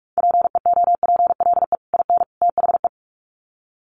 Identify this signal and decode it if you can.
Morse code, heard as a keyed tone, about 44 wpm, 700 Hz. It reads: PEOPLE IN THE
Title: Sets of 3 Words 45wpm